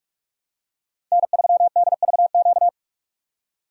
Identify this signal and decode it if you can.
Morse code, heard as a keyed tone, about 46 wpm, 700 Hz. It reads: N3DVQ
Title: USA Call Signs - 5 Characters - 3x Character Spacing - 45wpm